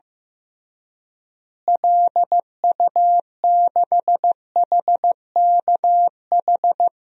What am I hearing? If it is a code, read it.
Morse code, heard as a keyed tone, about 15 wpm, 700 Hz. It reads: LU6HKH
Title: DXCC-100 Call Signs - 6 Characters - Repeated 3x - 15wpm